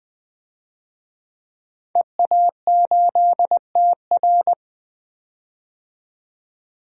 Morse code, heard as a keyed tone, about 21 wpm, 700 Hz. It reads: EA8TR
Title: DXCC-100 Call Signs - 5 Characters - 20wpm